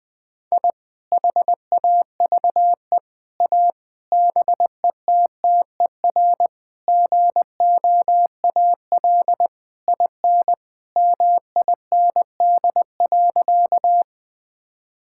Morse code, heard as a keyed tone, about 20 wpm, 700 Hz. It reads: I HAVE A BETTER GOAL IN MIND.
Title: Sentences from Top 1000 Words - Review Part 5 - 20wpm